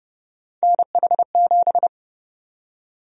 Morse code, heard as a keyed tone, about 31 wpm, 700 Hz. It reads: NH7
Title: USA Call Sign Prefixes - Repeated 2x - 30wpm